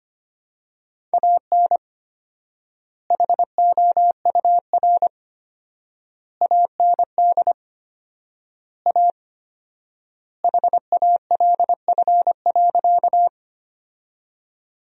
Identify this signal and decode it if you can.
Morse code, heard as a keyed tone, about 25 wpm, 700 Hz. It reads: AN HOUR AND A HALF.
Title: 4x Word-Spacing - Sentences from Top 400 Words - 25wpm